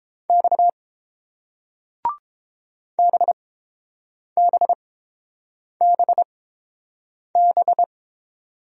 Morse code, heard as a keyed tone, about 30 wpm, spaced as for 7 wpm, 700 Hz. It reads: X BBBB